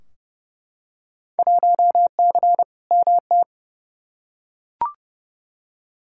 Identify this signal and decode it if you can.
Morse code, heard as a keyed tone, about 30 wpm, 700 Hz. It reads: 1C MT
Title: Field Day Exchanges Encore - 30wpm